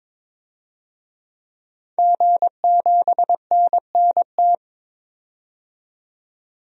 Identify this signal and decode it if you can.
Morse code, heard as a keyed tone, about 21 wpm, 700 Hz. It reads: G7NNT
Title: DXCC-100 Call Signs - 5 Characters - 22wpm